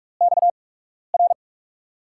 Morse code, heard as a keyed tone, about 44 wpm, spaced as for 12 wpm, 700 Hz. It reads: XR
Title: USA Call Signs - 5 Characters - 8x Character Spacing - 45wpm